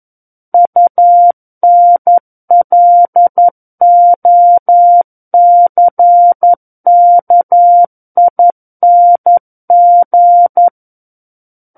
Morse code, heard as a keyed tone, about 11 wpm, 700 Hz. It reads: UNLOCKING
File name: Book One / 02-Chapter 2 - 11 WPM